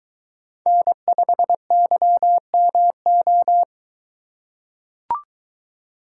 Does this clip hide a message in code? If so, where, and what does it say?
Yes: Morse code from 0.7 to 3.6 s, heard as a keyed tone, about 22 wpm, 700 Hz. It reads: N5YMO